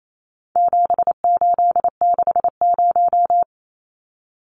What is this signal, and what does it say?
Morse code, heard as a keyed tone, about 28 wpm, 700 Hz. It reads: 7860
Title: Numbers - 4-Digit 28wpm